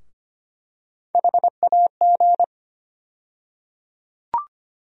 Morse code, heard as a keyed tone, about 25 wpm, 700 Hz. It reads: HAG
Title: USA Call Sign Postfixes - 3 Letters 25wpm